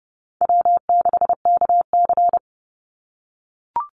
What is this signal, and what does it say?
Morse code, heard as a keyed tone, about 30 wpm, 700 Hz. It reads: W6KC
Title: Call Signs - 30wpm